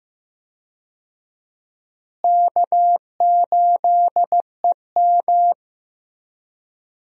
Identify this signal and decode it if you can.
Morse code, heard as a keyed tone, about 15 wpm, 700 Hz. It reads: K8EM
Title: USA Call Signs - 4 Characters - Repeated 3x - 15wpm